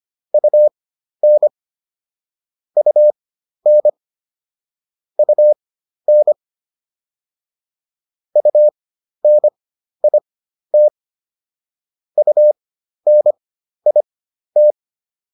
Morse code, heard as a keyed tone, about 25 wpm, spaced as for 12 wpm, 600 Hz. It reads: UN UN UN UNIT UNIT